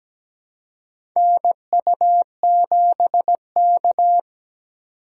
Morse code, heard as a keyed tone, about 17 wpm, 700 Hz. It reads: NU7K
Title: USA Call Signs - 3x Character Spacing - 17wpm